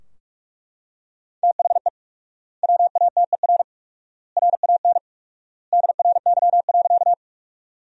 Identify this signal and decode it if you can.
Morse code, heard as a keyed tone, about 41 wpm, 700 Hz. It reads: THE WATER RAN DRY.
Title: 4x Word-Spacing - Sentences from Top 500 Words - 45wpm